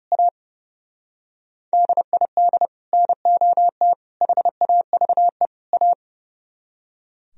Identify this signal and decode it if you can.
Morse code, heard as a keyed tone, about 30 wpm, 700 Hz. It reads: A DID NOT HAVE A
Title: Speed-Racing - Sets of 4 Words 30wpm